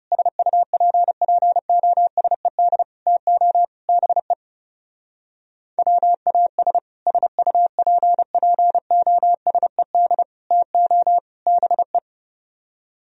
Morse code, heard as a keyed tone, about 39 wpm, 700 Hz. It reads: SUPPOSED TO BE WAS SUPPOSED TO BE